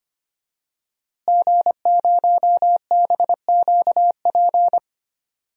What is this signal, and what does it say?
Morse code, heard as a keyed tone, about 25 wpm, 700 Hz. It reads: G0BQP
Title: DXCC-100 Call Signs - 5 Characters - Repeated 3x - 25wpm